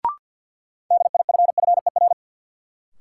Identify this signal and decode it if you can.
Morse code, heard as a keyed tone, about 52 wpm, 700 Hz. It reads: DIFFER
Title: Sentences from Top 1100 Words - New Word Drill Alt - 50wpm